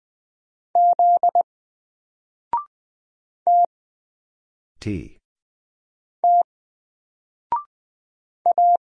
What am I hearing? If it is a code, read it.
Morse code, heard as a keyed tone, about 20 wpm, spaced as for 3 wpm, 700 Hz. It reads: Z TT A
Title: Single Letter-Number 20wpm